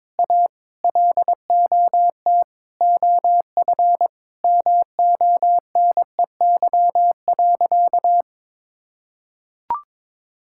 Morse code, heard as a keyed tone, about 22 wpm, 700 Hz. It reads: A LOT OF MONEY.
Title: Sentences from Top 1000 Words - Review Part 6 - 22wpm